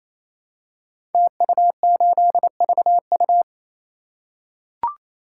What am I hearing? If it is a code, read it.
Morse code, heard as a keyed tone, about 27 wpm, 700 Hz. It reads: TU8VU
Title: Code Groups - 5 Letter:Numbers - 28wpm